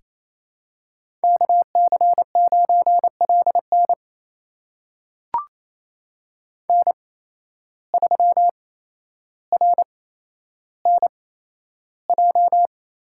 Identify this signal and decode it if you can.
Morse code, heard as a keyed tone, about 28 wpm, spaced as for 12 wpm, 700 Hz. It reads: KC9LN N3RNJ
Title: USA Call Signs - 5 Characters - 8x Character Spacing - 28wpm